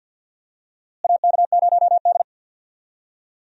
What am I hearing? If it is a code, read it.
Morse code, heard as a keyed tone, about 53 wpm, 700 Hz. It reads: AK0D
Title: USA Call Signs - 4 Characters - Repeated 3x - 50wpm